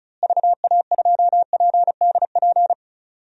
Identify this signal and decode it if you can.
Morse code, heard as a keyed tone, about 36 wpm, 700 Hz. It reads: VA2PDP
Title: DXCC-100 Call Signs - 6 Characters - Repeated 2x - 35wpm